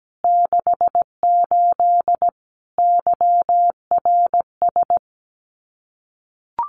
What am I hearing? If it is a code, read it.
Morse code, heard as a keyed tone, about 17 wpm, 700 Hz. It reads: 68 YRS
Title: Common QSO Phrases Encore 17wpm